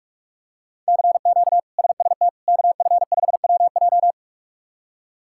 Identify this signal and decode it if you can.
Morse code, heard as a keyed tone, about 46 wpm, 700 Hz. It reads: CQ SST KF5WJ
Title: K1USN SST QSO Practice - 45wpm